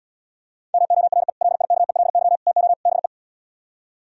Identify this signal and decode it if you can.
Morse code, heard as a keyed tone, about 77 wpm, 690 Hz. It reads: MORE LIKELY TO BE